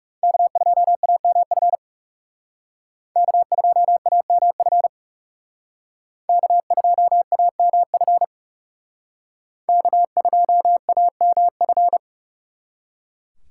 Morse code, heard as a keyed tone, about 36 wpm, 700 Hz. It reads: K2AMF K2AMF K2AMF K2AMF
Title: Speed Racing - USA 1x3 Call Signs 30wpm